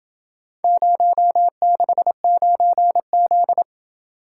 Morse code, heard as a keyed tone, about 28 wpm, 700 Hz. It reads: 069Z